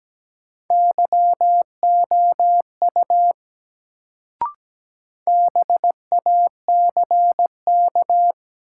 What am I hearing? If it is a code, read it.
Morse code, heard as a keyed tone, about 17 wpm, 700 Hz. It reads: YOU BACK
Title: Sets of 2 Words - New Word Drill - 17wpm